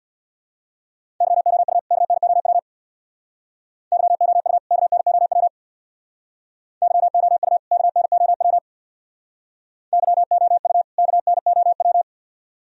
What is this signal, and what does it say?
Morse code, heard as a keyed tone, about 64 wpm, 700 Hz. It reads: YOU KNOW YOU KNOW YOU KNOW YOU KNOW